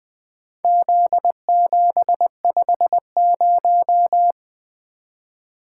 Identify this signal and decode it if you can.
Morse code, heard as a keyed tone, about 20 wpm, 700 Hz. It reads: Z750
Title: Four Letter-Number Combinations - 20wpm